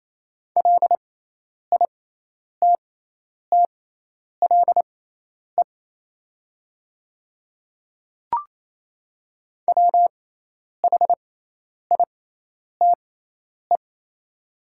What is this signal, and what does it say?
Morse code, heard as a keyed tone, about 27 wpm, spaced as for 10 wpm, 700 Hz. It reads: LITTLE WHITE
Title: 6x Character Spacing - Top 500 Words - 28wpm